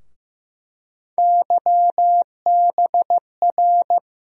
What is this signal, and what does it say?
Morse code, heard as a keyed tone, about 15 wpm, 700 Hz. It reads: YBR
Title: USA Call Sign Postfixes - 3 Letters 15wpm